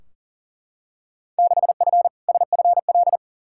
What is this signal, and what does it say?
Morse code, heard as a keyed tone, about 40 wpm, 700 Hz. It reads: 6F SFL
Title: Field Day Exchanges Encore - 40wpm